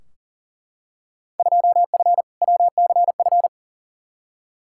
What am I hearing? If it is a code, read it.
Morse code, heard as a keyed tone, about 40 wpm, 700 Hz. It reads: 2F WCF
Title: Field Day Exchanges Encore - 40wpm